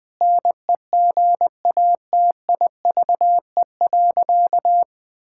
Morse code, heard as a keyed tone, about 20 wpm, 700 Hz. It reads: NEGATIVE.
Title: Sentences from Top 1000 Words - Review Part 6 - 20wpm